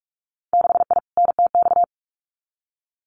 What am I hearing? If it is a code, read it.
Morse code, heard as a keyed tone, about 44 wpm, 700 Hz. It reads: 6I NTX
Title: Winter Field Day Exchanges 45wpm